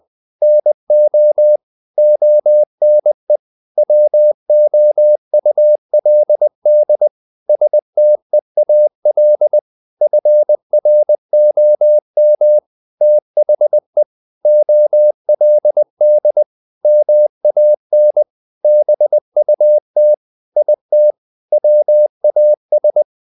Morse code, heard as a keyed tone, about 20 wpm, 600 Hz. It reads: NO ONE WOULD STEAL FROM THE OLD MAN BUT IT WAS
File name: the old man and the sea / the old man and the sea part 1 20wpm